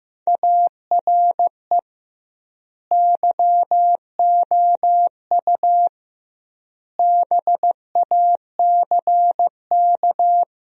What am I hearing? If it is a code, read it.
Morse code, heard as a keyed tone, about 15 wpm, 700 Hz. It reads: ARE YOU BACK
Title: 2x Word-Spacing - Sentences from Top 200 Words - 15wpm